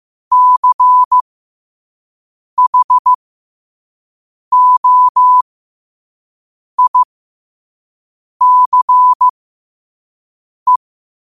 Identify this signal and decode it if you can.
Morse code, heard as a keyed tone, about 15 wpm, spaced as for 5 wpm, 1000 Hz. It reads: CHOICE
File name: qotdfiles / 5wpm